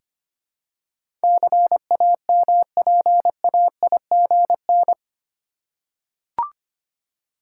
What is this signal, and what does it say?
Morse code, heard as a keyed tone, about 25 wpm, 700 Hz. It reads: CAMPAIGN
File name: 8 Letter Words - First 3 Letters Given - 25wpm